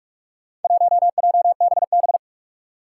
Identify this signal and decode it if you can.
Morse code, heard as a keyed tone, about 44 wpm, 700 Hz. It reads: 1JBB
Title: Four Challenging Letter-Number Combinations 45wpm